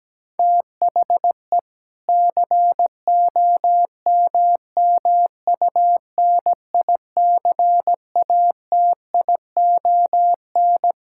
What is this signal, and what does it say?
Morse code, heard as a keyed tone, about 17 wpm, 700 Hz. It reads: THE COMMUNICATION
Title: Sentences from Top 1000 Words - Review Part 6 - 17wpm